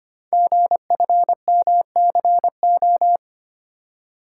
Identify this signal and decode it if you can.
Morse code, heard as a keyed tone, about 25 wpm, 700 Hz. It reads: GFMCO
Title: Code Groups - 5 Letters - 25wpm